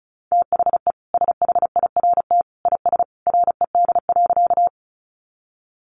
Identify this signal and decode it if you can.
Morse code, heard as a keyed tone, about 34 wpm, 700 Hz. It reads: THE SHIRT IS RED.
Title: Additional Easy Phrases 35wpm